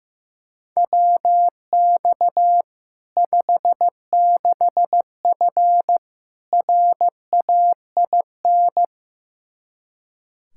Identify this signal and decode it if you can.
Morse code, heard as a keyed tone, about 16 wpm, 700 Hz. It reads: WX 56F RAIN
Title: Practice QSO - Exchange 2 - Info - 15wpm